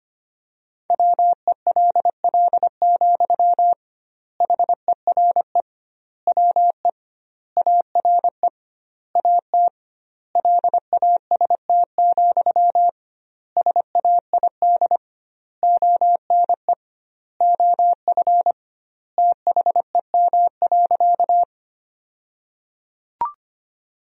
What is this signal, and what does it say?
Morse code, heard as a keyed tone, about 25 wpm, 700 Hz. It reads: WELL, HERE WE ARE AT LAST, SAID ONE OF THEM.
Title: 2x Word-Spacing - Sentences from Top 300 Words - 25wpm